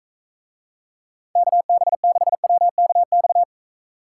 Morse code, heard as a keyed tone, about 44 wpm, 700 Hz. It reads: KB6WKX